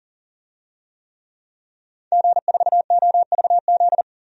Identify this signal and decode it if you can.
Morse code, heard as a keyed tone, about 40 wpm, 700 Hz. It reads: G4OVZ
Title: DXCC-100 Call Signs - 5 Characters - 40wpm